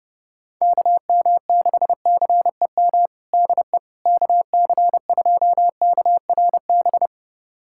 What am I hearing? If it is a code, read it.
Morse code, heard as a keyed tone, about 30 wpm, 700 Hz. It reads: KM6CEM DE KC2KRB
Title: USA Callsign DE Callsign - 6 Characters - 30wpm